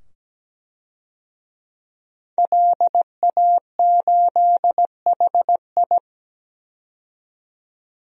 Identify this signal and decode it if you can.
Morse code, heard as a keyed tone, about 17 wpm, 700 Hz. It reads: LA8HI
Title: DXCC-100 Call Signs - 5 Characters - 17wpm